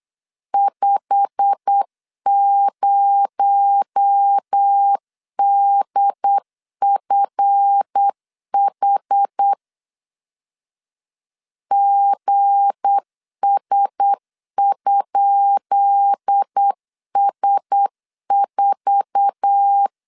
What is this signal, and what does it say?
Morse code, heard as a keyed tone, about 8 wpm, 800 Hz. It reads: 50DFH GS?S4
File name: LEZIONI / 10 SIGNS / 10D COMA 8wpm 3749